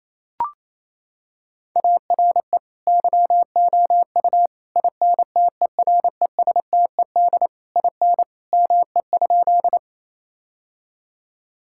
Morse code, heard as a keyed tone, about 28 wpm, 700 Hz. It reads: ARE YOU INTERESTED IN ME?
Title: Sentences from Top 1000 Words - Review Part 6 - 28wpm